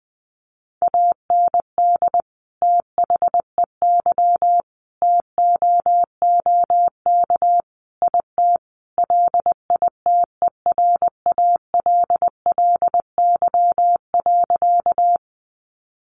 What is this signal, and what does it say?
Morse code, heard as a keyed tone, about 20 wpm, 700 Hz. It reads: AND THEY TOOK IT LITERALLY.
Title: Sentences from Top 1400 Words 20wpm